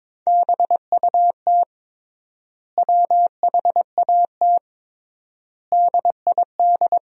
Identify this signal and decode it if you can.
Morse code, heard as a keyed tone, about 22 wpm, 700 Hz. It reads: BUT WHAT DID
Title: 3x Word-Spacing - Sentences from Top 100 Words - 22wpm